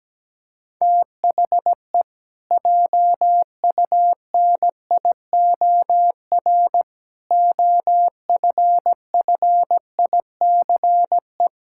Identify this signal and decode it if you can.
Morse code, heard as a keyed tone, about 17 wpm, 700 Hz. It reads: THE JUNIOR OFFICE
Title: Pangram Challenge v2 - 17wpm